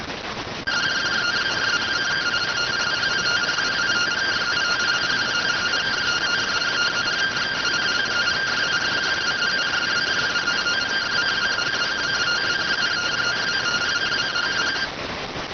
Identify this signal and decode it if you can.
Two-tone teleprinter signal, about 170 Hz shift.